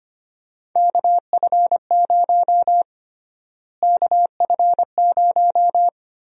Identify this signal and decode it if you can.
Morse code, heard as a keyed tone, about 25 wpm, 700 Hz. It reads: KF0 KF0
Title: USA Call Sign Prefixes - Repeated 2x - 25wpm